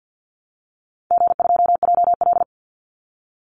Morse code, heard as a keyed tone, about 50 wpm, 700 Hz. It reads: G2JL
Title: Easy Call Signs Repeated 3x - 50wpm